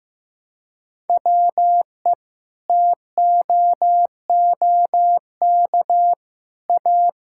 Morse code, heard as a keyed tone, about 16 wpm, 700 Hz. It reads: WE TOOK A
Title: Sentences from Top 1000 Words - Review Part 6 - 15wpm